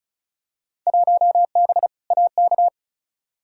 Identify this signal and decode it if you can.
Morse code, heard as a keyed tone, about 34 wpm, 700 Hz. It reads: 1B AK
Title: Field Day Exchanges Encore - 35wpm